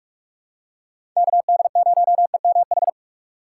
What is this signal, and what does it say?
Morse code, heard as a keyed tone, about 42 wpm, 700 Hz. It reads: KD0EMH
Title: USA Call Signs - 6 Characters - 4x Character Spacing - 45wpm